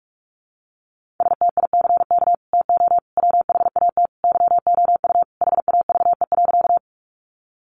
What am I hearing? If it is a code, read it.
Morse code, heard as a keyed tone, about 44 wpm, 700 Hz. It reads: STICK TO WHAT YOU HAVE.
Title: Sentences from Top 1200 Words 45wpm